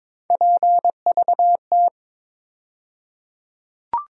Morse code, heard as a keyed tone, about 23 wpm, 700 Hz. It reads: PVT
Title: USA Call Sign Postfixes - 3 Letters 22wpm